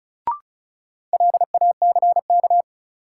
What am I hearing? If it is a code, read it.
Morse code, heard as a keyed tone, about 34 wpm, 700 Hz. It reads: LACK
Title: Sentences from Top 1100 Words - New Word Drill - 35wpm